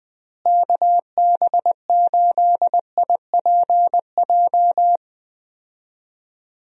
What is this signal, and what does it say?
Morse code, heard as a keyed tone, about 20 wpm, 700 Hz. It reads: KB8IPJ
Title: USA Call Signs - 6 Characters - Repeated 3x - 20wpm